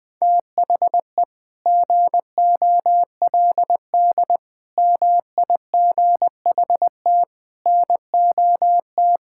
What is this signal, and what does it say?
Morse code, heard as a keyed tone, about 20 wpm, 700 Hz. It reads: THE GOLD MIGHT NOT
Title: Sentences from Top 1000 Words - Review Part 5 - 20wpm